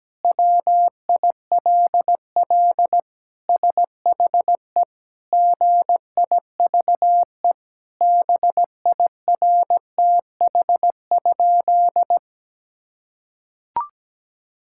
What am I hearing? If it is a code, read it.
Morse code, heard as a keyed tone, about 17 wpm, 700 Hz. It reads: WILL SHE GIVE BIRTH?
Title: Sentences from Top 1000 Words - Review Part 5 - 17wpm